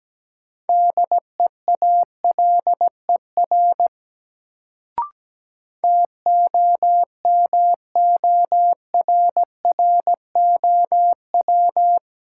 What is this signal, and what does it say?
Morse code, heard as a keyed tone, about 17 wpm, 700 Hz. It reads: DEALER TOMORROW
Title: Sentences from Top 1100 Words - New Word Drill Alt - 17wpm